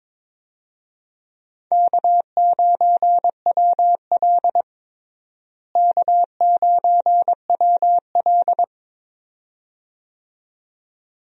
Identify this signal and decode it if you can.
Morse code, heard as a keyed tone, about 22 wpm, 700 Hz. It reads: K9WL K9WL
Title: USA Call Signs - 4 Characters - Repeated 2x - 22wpm